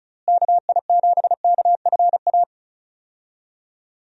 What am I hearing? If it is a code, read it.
Morse code, heard as a keyed tone, about 36 wpm, 700 Hz. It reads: KI7KFA
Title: USA Call Signs - 6 Characters - Repeated 3x - 35wpm